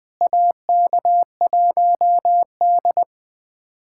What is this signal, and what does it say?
Morse code, heard as a keyed tone, about 21 wpm, 700 Hz. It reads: AK1D
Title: USA Callsign DE Callsign - 4 Characters - 20wpm